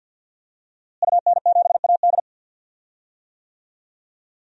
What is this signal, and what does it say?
Morse code, heard as a keyed tone, about 46 wpm, 700 Hz. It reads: UN7AD
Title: DXCC-100 Call Signs - 5 Characters - Repeated 3x - 50wpm